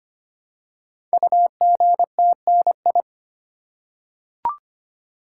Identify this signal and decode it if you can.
Morse code, heard as a keyed tone, about 25 wpm, 700 Hz. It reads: UGTNI
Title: Code Groups - 5 Letters - 25wpm